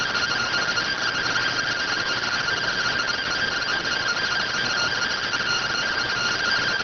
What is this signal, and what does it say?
Two-tone teleprinter signal, about 170 Hz shift.